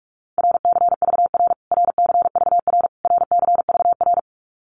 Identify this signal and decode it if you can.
Morse code, heard as a keyed tone, about 44 wpm, 700 Hz. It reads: RCVR RCVR RCVR
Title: Abbreviations Repeated 3x - 45wpm